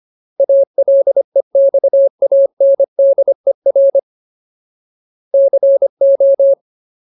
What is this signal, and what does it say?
Morse code, heard as a keyed tone, about 25 wpm, 550 Hz. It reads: ALEXANDER CO
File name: Ultimate Word List / 25WPM Ultimate